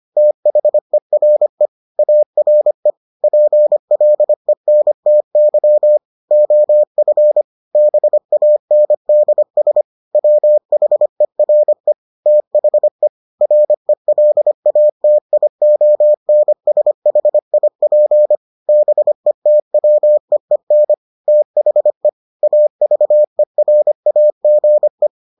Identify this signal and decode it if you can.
Morse code, heard as a keyed tone, about 25 wpm, 600 Hz. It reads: THERE ARE PLENTY OF BANDS WHERE THE RELATIONSHIP BETWEEN THE AVERAGE